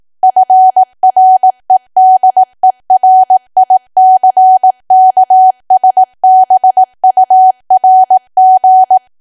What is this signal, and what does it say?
Morse code, heard as a keyed tone, about 18 wpm, 750 Hz. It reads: FREDERICKSBURG
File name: cwa / W1AW-Level2 / 18WPM